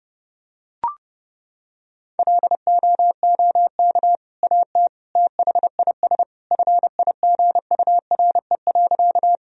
Morse code, heard as a keyed tone, about 30 wpm, 700 Hz. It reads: LOOK AT THIS FIGURE.
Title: Sentences from Top 1000 Words - Review Part 2 - 30wpm